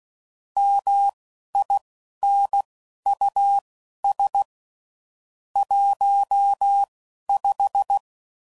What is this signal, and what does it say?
Morse code, heard as a keyed tone, about 16 wpm, spaced as for 11 wpm, 800 Hz. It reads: MINUS 15
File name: CW Code Groups @10WPM / QSOs / QSO WD0FBT